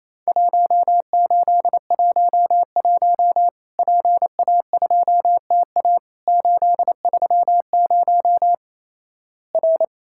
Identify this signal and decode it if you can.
Morse code, heard as a keyed tone, about 28 wpm, 700 Hz. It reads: 1811 PA2TA 830 R
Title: Worked All Europe DX Contest - QTC Exchange - 28wpm